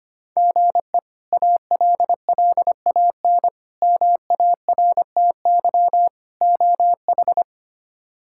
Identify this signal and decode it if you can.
Morse code, heard as a keyed tone, about 25 wpm, 700 Hz. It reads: GE ALLAN MARTY OH
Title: K1USN SST QSO Practice - 25wpm